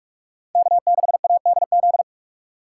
Morse code, heard as a keyed tone, about 45 wpm, 700 Hz. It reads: K6ADZ
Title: USA Call Signs - 3x Character Spacing - 45wpm